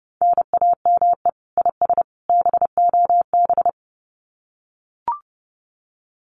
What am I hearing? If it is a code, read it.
Morse code, heard as a keyed tone, about 30 wpm, 700 Hz. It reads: NAME IS BOB
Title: Common QSO Phrases 30wpm